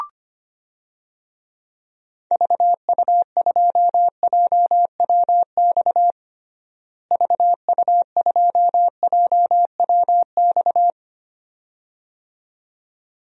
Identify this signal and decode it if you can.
Morse code, heard as a keyed tone, about 25 wpm, 700 Hz. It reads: VU2JWX VU2JWX
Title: DXCC-100 Call Signs - 6 Characters - Repeated 2x - 25wpm